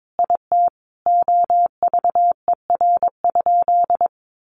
Morse code, heard as a keyed tone, about 22 wpm, 700 Hz. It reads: IT OVER?
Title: Sentences from Top 100 Words 22wpm